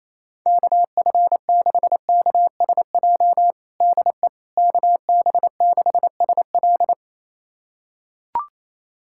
Morse code, heard as a keyed tone, about 28 wpm, 700 Hz. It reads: KF6KSJ DE KB6SL
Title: USA Callsign DE Callsign 28wpm